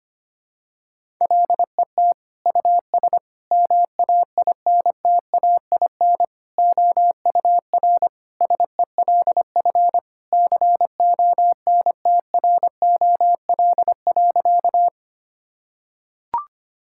Morse code, heard as a keyed tone, about 25 wpm, 700 Hz. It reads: LET US MAINTAIN OUR SELF CONTROL.
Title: Sentences from Top 1000 Words - Review Part 5 - 25wpm